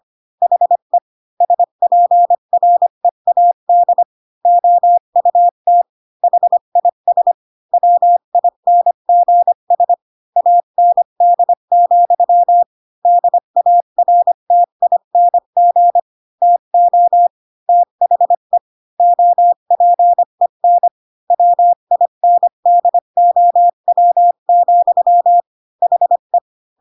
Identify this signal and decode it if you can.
Morse code, heard as a keyed tone, about 25 wpm, 700 Hz. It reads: HE SPREAD OUT HIS WINGS AND, DARTING TO THE OPEN WINDOW, HE
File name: Pinocchio / Pinocchio ch5 20 25 ik1hge